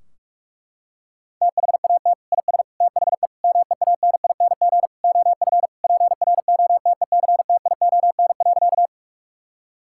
Morse code, heard as a keyed tone, about 43 wpm, 700 Hz. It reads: THAT IS THE MEANING OF PROTECTION.
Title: Sentences from Top 1600 Words 45wpm